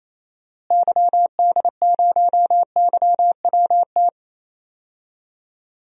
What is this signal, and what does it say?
Morse code, heard as a keyed tone, about 28 wpm, 700 Hz. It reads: YD0YWT
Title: DXCC-100 Call Signs - 6 Characters - Repeated 3x - 28wpm